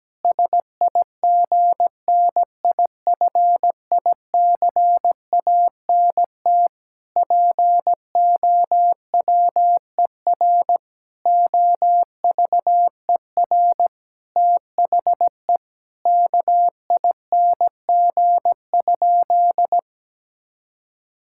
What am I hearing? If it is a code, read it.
Morse code, heard as a keyed tone, about 17 wpm, 700 Hz. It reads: SIGNIFICANT POWER OVER THE KING?
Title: Sentences from Top 1000 Words - Review Part 6 - 17wpm